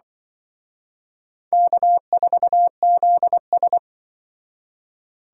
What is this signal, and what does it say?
Morse code, heard as a keyed tone, about 24 wpm, 700 Hz. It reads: K4ZS
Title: Speed Racing - USA 1x2 Call Signs 20wpm